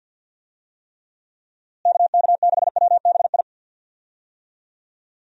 Morse code, heard as a keyed tone, about 51 wpm, 700 Hz. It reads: KK6WBI
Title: USA Call Signs - 6 Characters - Repeated 3x - 50wpm